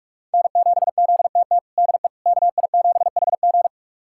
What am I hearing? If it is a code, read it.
Morse code, heard as a keyed tone, about 42 wpm, 700 Hz. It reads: N7ZTT DE KI7HG
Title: USA Callsign DE Callsign - 5 Characters - 45wpm